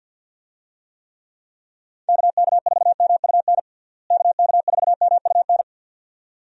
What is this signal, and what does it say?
Morse code, heard as a keyed tone, about 50 wpm, 700 Hz. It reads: KK4MUN KK4MUN
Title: USA Call Signs - 6 Characters - Repeated 2x - 50wpm